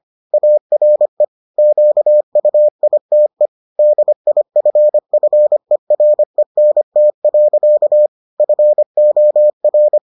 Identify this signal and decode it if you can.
Morse code, heard as a keyed tone, about 25 wpm, 600 Hz. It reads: ARE QUITE DIFFERENT. FOR